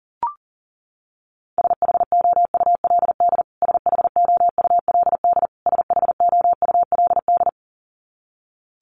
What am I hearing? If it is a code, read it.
Morse code, heard as a keyed tone, about 40 wpm, 700 Hz. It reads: SHOULD SHOULD SHOULD
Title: Top 500 Words Repeated 40wpm